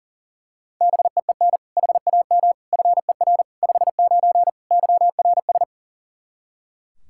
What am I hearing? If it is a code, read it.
Morse code, heard as a keyed tone, about 37 wpm, 700 Hz. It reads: BEEN HAM FER 59 YRS
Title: Practice QSO - Exchange 3 - More Info - 40wpm